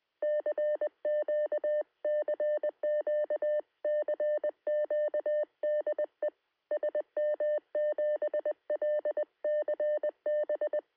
Morse code, heard as a keyed tone, about 20 wpm, 600 Hz. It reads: CQ CQ CQ DE SM7LCB
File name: sm7lcb / 2115z g3xdy 23cm